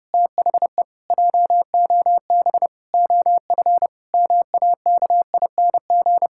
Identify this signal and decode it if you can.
Morse code, heard as a keyed tone, about 32 wpm, 700 Hz. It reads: THE JOB OF MAKING
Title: Pangram Challenge v2 - 30wpm